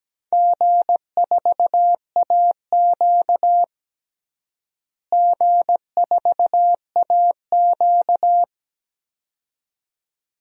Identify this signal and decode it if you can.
Morse code, heard as a keyed tone, about 17 wpm, 700 Hz. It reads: G4AQ G4AQ
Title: DXCC-100 Call Signs - 4 Characters - Repeated 2x - 17wpm